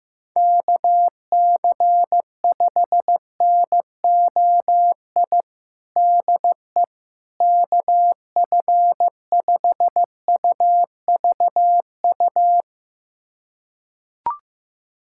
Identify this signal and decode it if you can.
Morse code, heard as a keyed tone, about 16 wpm, 700 Hz. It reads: KC5NOI DE KF5UVU
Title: USA Callsign DE Callsign - 6 Characters - 15wpm